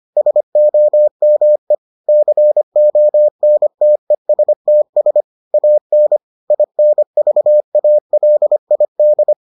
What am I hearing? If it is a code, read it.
Morse code, heard as a keyed tone, about 25 wpm, 600 Hz. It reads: SOME CONTESTS AN INVALID